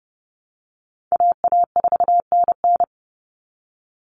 Morse code, heard as a keyed tone, about 30 wpm, 700 Hz. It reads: AA4NN
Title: Call Signs Repeated 3x - 30wpm